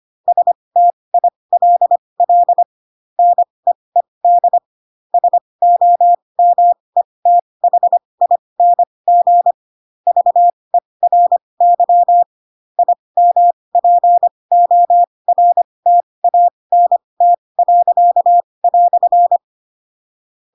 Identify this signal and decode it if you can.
Morse code, heard as a keyed tone, about 25 wpm, spaced as for 20 wpm, 700 Hz. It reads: STILL NEED SOMETHING VERY IMPORTANT."
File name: Pinocchio / Pinocchio ch8 15 25 ik1hge